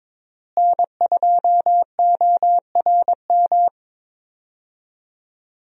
Morse code, heard as a keyed tone, about 21 wpm, 700 Hz. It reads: N2ORM
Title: USA Call Signs - 5 Characters - Repeated 3x - 22wpm